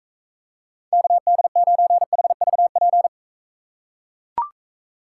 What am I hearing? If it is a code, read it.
Morse code, heard as a keyed tone, about 42 wpm, 700 Hz. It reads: KD9HVP